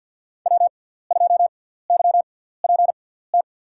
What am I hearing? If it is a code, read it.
Morse code, heard as a keyed tone, about 47 wpm, spaced as for 17 wpm, 700 Hz. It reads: W2YPT
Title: USA Call Signs - 6x Character Spacing - 50wpm